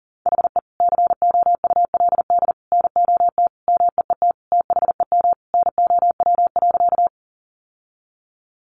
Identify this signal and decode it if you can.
Morse code, heard as a keyed tone, about 40 wpm, 700 Hz. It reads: HE COULD NOT MEET THEM NOW.
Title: Sentences from Top 1100 Words 40wpm